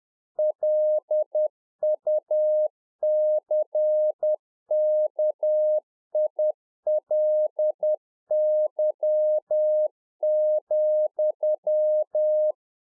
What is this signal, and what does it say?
Morse code, heard as a keyed tone, about 10 wpm, 610 Hz. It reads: LUCKILY,
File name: UNFOGETTABLE QSO 10WPM